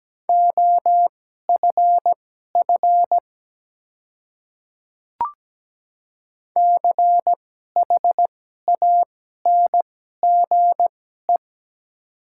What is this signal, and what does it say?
Morse code, heard as a keyed tone, about 17 wpm, spaced as for 12 wpm, 700 Hz. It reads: OFF CHANGE
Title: 2x Character Spacing - Top 300 Words - 17wpm